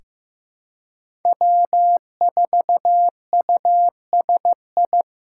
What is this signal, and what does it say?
Morse code, heard as a keyed tone, about 15 wpm, 700 Hz. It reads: W4USI
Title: K1USN SST QSO Practice - 15wpm